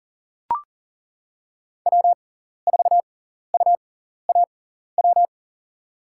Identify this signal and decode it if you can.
Morse code, heard as a keyed tone, about 38 wpm, spaced as for 14 wpm, 700 Hz. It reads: W4UAW
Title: USA Call Signs - 5 Characters - 6x Character Spacing - 40wpm